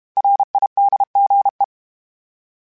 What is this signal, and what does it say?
Morse code, heard as a keyed tone, about 32 wpm, 800 Hz. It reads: RIDGE
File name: fromSMs / FromSMs mp3s / Cities@30